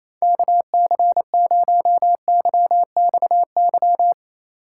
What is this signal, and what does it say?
Morse code, heard as a keyed tone, about 34 wpm, 700 Hz. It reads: KC0YXY
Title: Speed Racing - USA Call Signs 28wpm